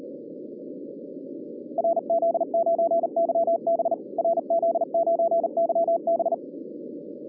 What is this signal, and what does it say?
Morse code, heard as a keyed tone, about 39 wpm, 660 Hz, with noise. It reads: RZ9YB RZ9YB
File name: 2011 bcc-challenge sonntag tape